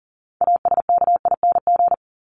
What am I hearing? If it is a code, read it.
Morse code, heard as a keyed tone, about 40 wpm, 700 Hz. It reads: ASKING
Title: ING Suffix - Top 100 Words with Base Words 40wpm